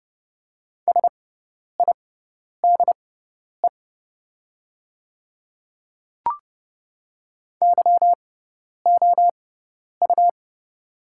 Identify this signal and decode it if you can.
Morse code, heard as a keyed tone, about 30 wpm, spaced as for 10 wpm, 700 Hz. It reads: SIDE YOU
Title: 6x Character Spacing - Top 200 Words - 30wpm